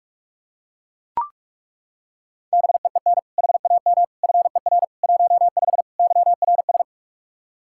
Morse code, heard as a keyed tone, about 41 wpm, 700 Hz. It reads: BEEN HAM FER 15 YRS
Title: Practice QSO - Exchange 3 - More Info - 45wpm